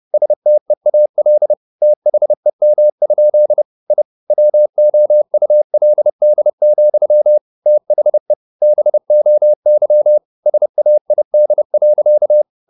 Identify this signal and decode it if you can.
Morse code, heard as a keyed tone, about 30 wpm, 600 Hz. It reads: STEAL THEM? I WOULD, THE BOY SAID.
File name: the old man and the sea / the old man and the sea 1 30wpm